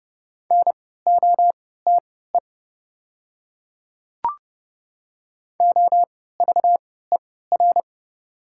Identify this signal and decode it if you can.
Morse code, heard as a keyed tone, about 30 wpm, spaced as for 17 wpm, 690 Hz. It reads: NOTE OVER
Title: 3x Character Spacing - Top 500 Words - 30wpm